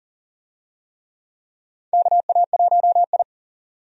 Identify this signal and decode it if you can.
Morse code, heard as a keyed tone, about 38 wpm, 700 Hz. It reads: KA1I
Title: USA Call Signs - 4 Characters - 40wpm